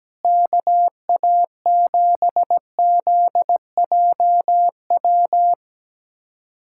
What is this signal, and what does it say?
Morse code, heard as a keyed tone, about 17 wpm, 700 Hz. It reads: KA7ZJW
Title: USA Callsign DE Callsign - 6 Characters - 17wpm